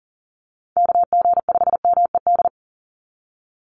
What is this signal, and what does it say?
Morse code, heard as a keyed tone, about 40 wpm, 700 Hz. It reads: KG5MED
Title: USA Call Signs 40wpm